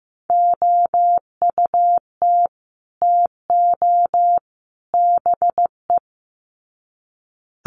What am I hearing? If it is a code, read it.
Morse code, heard as a keyed tone, about 15 wpm, 700 Hz. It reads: OUT TO BE
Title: Sets of 3 Words 15wpm